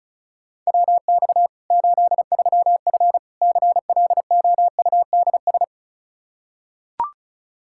Morse code, heard as a keyed tone, about 36 wpm, 700 Hz. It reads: WX 83F CLOUDS
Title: Practice QSO - Exchange 2 - Info - 35wpm